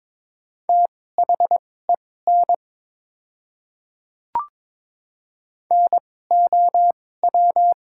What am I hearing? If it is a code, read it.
Morse code, heard as a keyed tone, about 22 wpm, spaced as for 16 wpm, 690 Hz. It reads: THEN NOW
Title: 2x Character Spacing - Top 100 Words - 22wpm